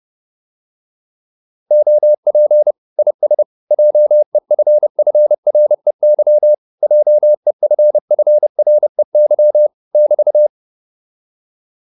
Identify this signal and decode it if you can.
Morse code, heard as a keyed tone, about 30 wpm, 600 Hz. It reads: OP IS JEFFREY JEFFREY <BT>